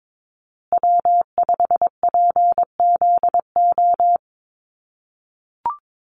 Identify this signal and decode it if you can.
Morse code, heard as a keyed tone, about 22 wpm, 700 Hz. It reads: W5PZO
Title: USA Call Signs Repeated 2x 22wpm